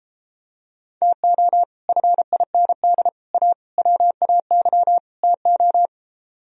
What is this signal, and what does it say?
Morse code, heard as a keyed tone, about 33 wpm, 700 Hz. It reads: TO FIND A WAY TO